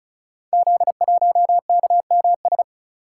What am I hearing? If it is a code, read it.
Morse code, heard as a keyed tone, about 36 wpm, 700 Hz. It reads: Z1KMS
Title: Code Groups - 5 Letter:Numbers - 35wpm